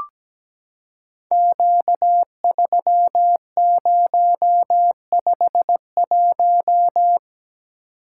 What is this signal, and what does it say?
Morse code, heard as a keyed tone, about 17 wpm, 700 Hz. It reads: Q3051
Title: Code Groups - 5 Letter:Numbers - 17wpm